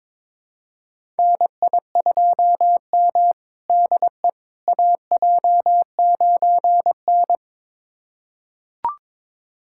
Morse code, heard as a keyed tone, about 22 wpm, 700 Hz. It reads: NI2M DE AJ9N
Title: USA Callsign DE Callsign - 4 Characters - 22wpm